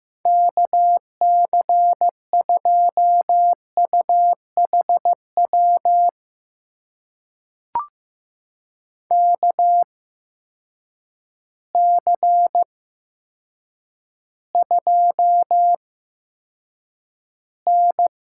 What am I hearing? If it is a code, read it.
Morse code, heard as a keyed tone, about 15 wpm, spaced as for 8 wpm, 700 Hz. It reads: KC2UHW KC2N
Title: USA Call Signs - 8x Character Spacing - 15wpm